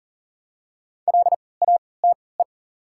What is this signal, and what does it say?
Morse code, heard as a keyed tone, about 40 wpm, spaced as for 23 wpm, 690 Hz. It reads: LATE
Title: 3x Character Spacing - Top 300 Words - 40wpm